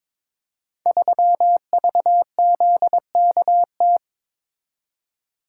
Morse code, heard as a keyed tone, about 22 wpm, 700 Hz. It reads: 3VZKT
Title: Code Groups - 5 Letter:Numbers - 22wpm